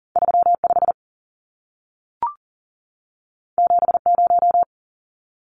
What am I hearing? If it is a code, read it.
Morse code, heard as a keyed tone, about 40 wpm, 700 Hz. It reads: 35 70
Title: Numbers - 2-Digit 40wpm